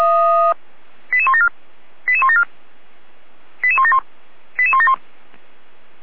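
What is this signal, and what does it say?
Five-tone selective call (ZVEI), heard as quick runs of five tones, tones of about 70 ms. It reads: 88174 88174 88171 88171
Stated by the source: ZVEI five-tone call 88199 received on SDR, Feuerwehr Orlamünde.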